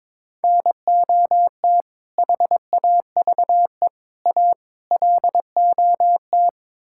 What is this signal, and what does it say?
Morse code, heard as a keyed tone, about 22 wpm, 700 Hz. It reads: NOT HAVE A LOT
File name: practice / 22wpm / Build A Phrase - 5-Word Phrases - 22wpm